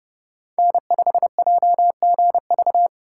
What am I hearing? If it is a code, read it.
Morse code, heard as a keyed tone, about 32 wpm, 700 Hz. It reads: N5JGV
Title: USA Call Signs - 5 Characters - 4x Character Spacing - 30wpm